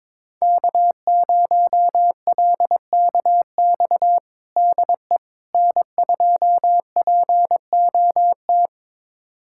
Morse code, heard as a keyed tone, about 22 wpm, 700 Hz. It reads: K0LKX DE N2POT
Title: USA Callsign DE Callsign - 5 Characters - 22wpm